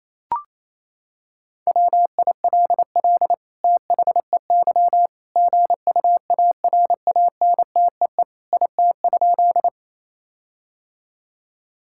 Morse code, heard as a keyed tone, about 28 wpm, 700 Hz. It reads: WILL THEY GUARANTEE IT?
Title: Sentences from Top 1600 Words 28wpm